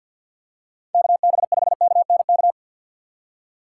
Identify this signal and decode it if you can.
Morse code, heard as a keyed tone, about 48 wpm, 700 Hz. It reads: KB5KNK
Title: USA Call Signs - 3x Character Spacing - 50wpm